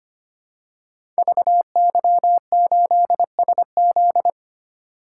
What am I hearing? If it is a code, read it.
Morse code, heard as a keyed tone, about 25 wpm, 700 Hz. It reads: VY8SZ
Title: Code Groups - 5 Letter:Numbers - 25wpm